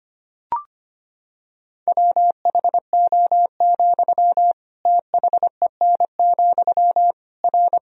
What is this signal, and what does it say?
Morse code, heard as a keyed tone, about 25 wpm, 700 Hz. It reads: WHO, THEN, R
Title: Sentences from Top 100 Words and Abbreviations 25wpm